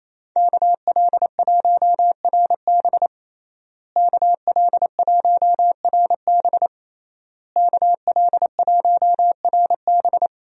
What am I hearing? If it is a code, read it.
Morse code, heard as a keyed tone, about 27 wpm, 700 Hz. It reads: KL1RB KL1RB KL1RB
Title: USA Call Signs - 5 Characters - Repeated 3x - 28wpm